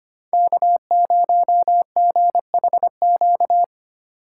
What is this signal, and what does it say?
Morse code, heard as a keyed tone, about 25 wpm, 700 Hz. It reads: K0GHQ
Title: USA Call Signs - 5 Characters - 6x Character Spacing - 25wpm